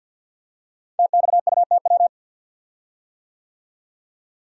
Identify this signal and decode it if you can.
Morse code, heard as a keyed tone, about 46 wpm, 700 Hz. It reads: TXUTW
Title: Code Groups - 5 Letters - 50wpm